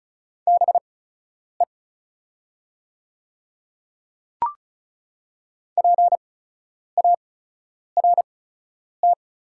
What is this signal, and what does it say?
Morse code, heard as a keyed tone, about 36 wpm, spaced as for 10 wpm, 700 Hz. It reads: BE PART
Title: 8x Character Spacing - Top 200 Words - 35wpm